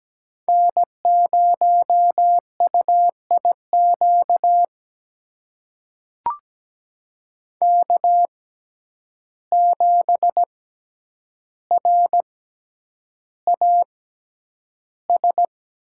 Morse code, heard as a keyed tone, about 17 wpm, spaced as for 9 wpm, 700 Hz. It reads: N0UIQ K7RAS
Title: USA Call Signs - 5 Characters - 6x Character Spacing - 17wpm